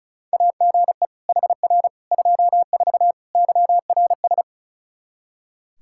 Morse code, heard as a keyed tone, about 36 wpm, 700 Hz. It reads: AGE HR 24 YRS
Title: K1USN SST QSO Practice 35wpm